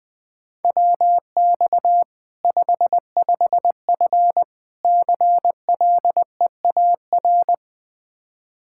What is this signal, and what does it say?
Morse code, heard as a keyed tone, about 21 wpm, 700 Hz. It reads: WX 55F CLEAR
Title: Practice QSO - Exchange 2 - Info - 20wpm